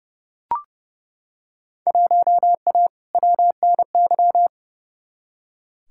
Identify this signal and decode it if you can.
Morse code, heard as a keyed tone, about 31 wpm, 700 Hz. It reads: 1A WNY
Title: Field Day Exchanges Encore - 30wpm